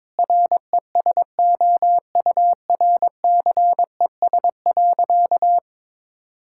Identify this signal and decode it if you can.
Morse code, heard as a keyed tone, about 22 wpm, 700 Hz. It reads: RESOURCES.
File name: practice / 22wpm / Sentences with 2-Word Phrases - 22wpm